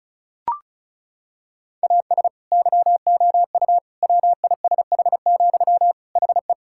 Morse code, heard as a keyed tone, about 34 wpm, 700 Hz. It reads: AS YOU WISH, HE
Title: Sentences from Top 1600 Words 35wpm